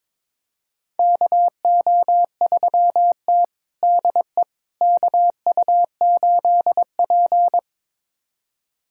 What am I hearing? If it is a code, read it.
Morse code, heard as a keyed tone, about 22 wpm, 700 Hz. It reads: KO3T DE KU8P
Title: USA Callsign DE Callsign - 4 Characters - 22wpm